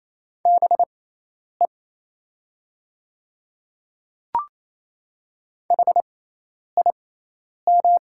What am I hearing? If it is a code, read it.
Morse code, heard as a keyed tone, about 28 wpm, spaced as for 10 wpm, 700 Hz. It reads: BE HIM
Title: 6x Character Spacing - Top 100 Words - 28wpm